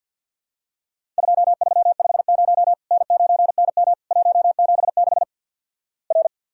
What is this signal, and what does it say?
Morse code, heard as a keyed tone, about 50 wpm, 700 Hz. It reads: 2350 N9NM 176 R
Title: Worked All Europe DX Contest - QTC Exchange with Repeats - 50wpm